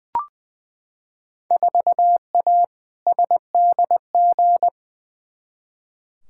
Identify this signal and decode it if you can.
Morse code, heard as a keyed tone, about 20 wpm, 700 Hz. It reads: 4A SDG
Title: Field Day Exchanges Encore - 20wpm